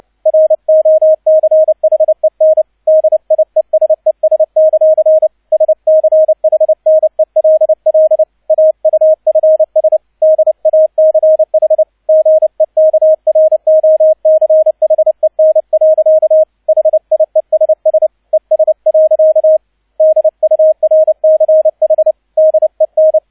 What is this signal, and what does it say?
Morse code, heard as a keyed tone, about 29 wpm, 630 Hz, with radio noise. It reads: ROCHEN DIESES; SCHNELL AUFS DACH GEKROCHEN. HIESS ES. DURCH DEN
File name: 002 Zweiter Streich 30wpm